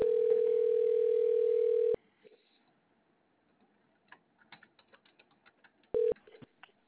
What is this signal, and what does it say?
Ringback tone (North America), 440 + 480 Hz.